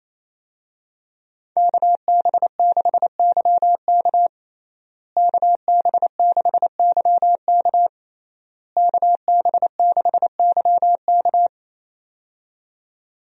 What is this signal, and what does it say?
Morse code, heard as a keyed tone, about 28 wpm, 700 Hz. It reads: KB6YK KB6YK KB6YK
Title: USA Call Signs - 5 Characters - Repeated 3x - 28wpm